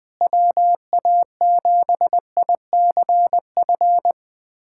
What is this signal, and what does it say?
Morse code, heard as a keyed tone, about 21 wpm, 700 Hz. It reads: WA7ICF
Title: USA Callsign DE Callsign - 6 Characters - 20wpm